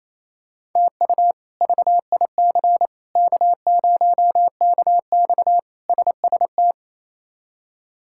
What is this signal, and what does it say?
Morse code, heard as a keyed tone, about 28 wpm, 700 Hz. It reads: TU VIC K0KX SST
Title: K1USN SST QSO Practice - 28wpm